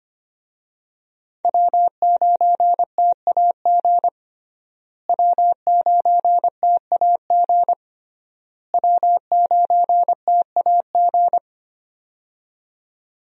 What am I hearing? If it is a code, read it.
Morse code, heard as a keyed tone, about 25 wpm, 700 Hz. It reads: W9TAG W9TAG W9TAG
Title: USA Call Signs - 5 Characters - Repeated 3x - 25wpm